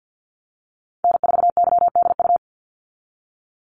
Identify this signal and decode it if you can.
Morse code, heard as a keyed tone, about 50 wpm, 700 Hz. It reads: N4YDU
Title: Call Signs - 50wpm